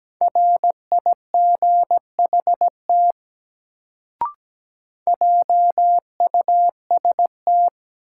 Morse code, heard as a keyed tone, about 17 wpm, 700 Hz. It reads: RIGHT JUST
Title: Sentences from Top 200 Words - New Word Drill - 17wpm